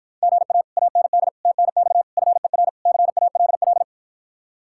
Morse code, heard as a keyed tone, about 52 wpm, 700 Hz. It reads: GA AND TNX FER CALL